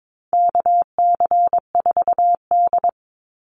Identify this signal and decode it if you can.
Morse code, heard as a keyed tone, about 22 wpm, 700 Hz. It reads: KC4D
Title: Call Signs Repeated 3x - 22wpm